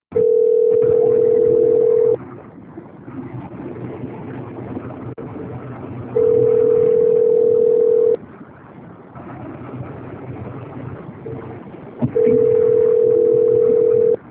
Ringback tone (North America), 440 + 480 Hz.
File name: out- - - - - .12343